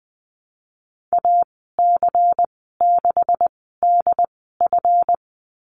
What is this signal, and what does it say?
Morse code, heard as a keyed tone, about 20 wpm, spaced as for 15 wpm, 700 Hz. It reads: AC6DF
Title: 167 Lesson 27g - Number 7 - Call Signs 20wpm